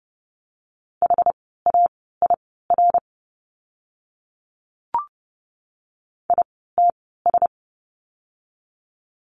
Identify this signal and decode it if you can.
Morse code, heard as a keyed tone, about 30 wpm, spaced as for 17 wpm, 700 Hz. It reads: HAIR ITS
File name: Lesson 17f - Letter u - Words 30wpm